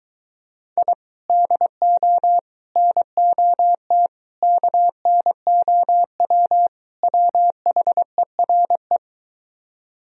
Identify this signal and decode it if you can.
Morse code, heard as a keyed tone, about 23 wpm, 700 Hz. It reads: I DO NOT KNOW WHERE